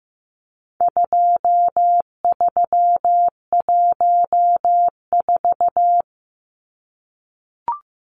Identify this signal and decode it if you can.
Morse code, heard as a keyed tone, about 15 wpm, 700 Hz. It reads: 2314
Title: Numbers - 4-Digit 15wpm